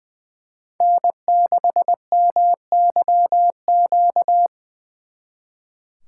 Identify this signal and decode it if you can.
Morse code, heard as a keyed tone, about 24 wpm, 700 Hz. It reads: N6MYQ
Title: Speed Racing - USA Call Signs 20wpm